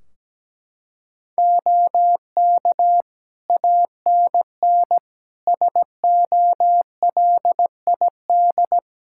Morse code, heard as a keyed tone, about 17 wpm, 700 Hz. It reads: OK ANN SOLID
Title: K1USN SST QSO Practice 17wpm